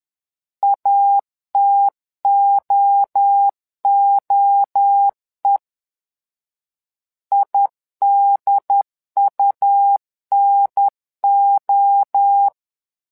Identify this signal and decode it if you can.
Morse code, heard as a keyed tone, about 10 wpm, 800 Hz. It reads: ATOOE IDUNO
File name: LEZIONI / 3 MWRO / 3 MWRO 10wpm 2148